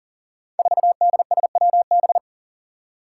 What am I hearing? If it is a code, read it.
Morse code, heard as a keyed tone, about 40 wpm, 700 Hz. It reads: 4DSWB
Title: Five Challenging Letter-Number Combinations 40wpm